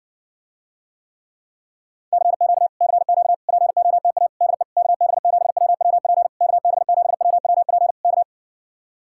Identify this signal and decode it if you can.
Morse code, heard as a keyed tone, about 59 wpm, 700 Hz. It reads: CQ CQ POTA DE KB7WWP KB7WWP K